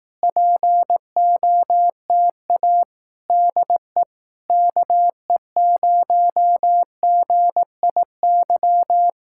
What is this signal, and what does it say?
Morse code, heard as a keyed tone, about 18 wpm, 700 Hz. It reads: POTA DE KE0GIY
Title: Speed-Racing Common QSO Phrases Encore - 15wpm